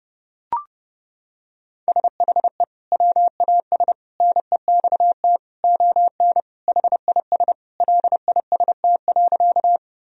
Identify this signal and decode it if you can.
Morse code, heard as a keyed tone, about 30 wpm, 700 Hz. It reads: SHE WAS NEXT ON HIS LIST.
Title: Sentences from Top 1000 Words - Review Part 2 - 30wpm